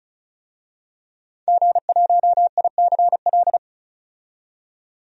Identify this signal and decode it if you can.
Morse code, heard as a keyed tone, about 34 wpm, 700 Hz. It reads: G1ICL
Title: DXCC-100 Call Signs - 5 Characters - 35wpm